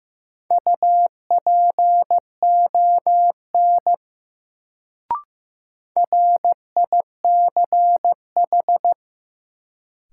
Morse code, heard as a keyed tone, about 16 wpm, 700 Hz. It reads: UPON RICH
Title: Sentences from Top 1100 Words - New Word Drill Alt - 15wpm